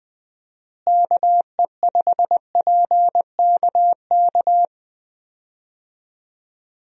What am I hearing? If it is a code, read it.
Morse code, heard as a keyed tone, about 20 wpm, 700 Hz. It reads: KE5PKK
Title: USA Call Signs - 6 Characters - Repeated 2x - 20wpm